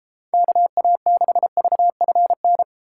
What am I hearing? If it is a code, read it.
Morse code, heard as a keyed tone, about 32 wpm, 700 Hz. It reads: KA6VFN